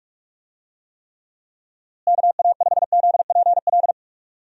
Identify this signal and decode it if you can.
Morse code, heard as a keyed tone, about 45 wpm, 700 Hz. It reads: KA5ZPL
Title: USA Call Signs - 6 Characters - Repeated 2x - 45wpm